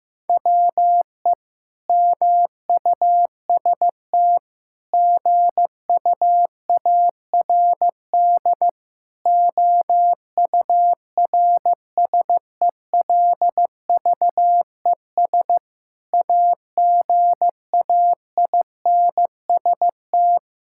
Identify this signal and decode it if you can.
Morse code, heard as a keyed tone, about 16 wpm, 700 Hz. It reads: WE MUST GUARD OURSELVES AGAINST
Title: Sentences from Top 1600 Words 15wpm